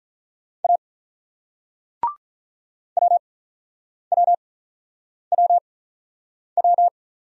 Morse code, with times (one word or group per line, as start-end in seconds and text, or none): 0.6-0.8 A
3.0-6.9 WWWW